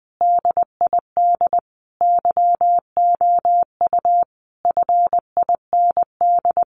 Morse code, heard as a keyed tone, about 20 wpm, 700 Hz. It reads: DID YOU FIND
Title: Sentences from Top 200 Words 20wpm